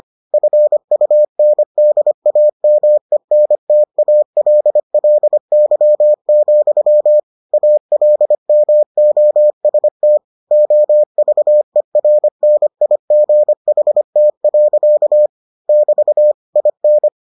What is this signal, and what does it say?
Morse code, heard as a keyed tone, about 25 wpm, 600 Hz. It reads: FUNDAMENTALLY, ALMOST OVERNIGHT. <BT> IN